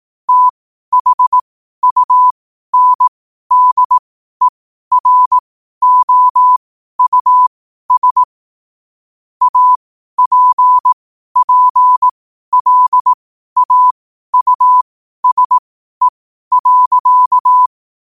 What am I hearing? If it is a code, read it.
Morse code, heard as a keyed tone, about 18 wpm, spaced as for 12 wpm, 1000 Hz. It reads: THUNDEROUS APPLAUSE.
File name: qotdfiles / 12wpm